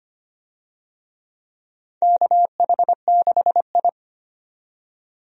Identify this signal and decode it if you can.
Morse code, heard as a keyed tone, about 25 wpm, 700 Hz. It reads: KH6I
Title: DXCC-100 Call Signs - 4 Characters - 25wpm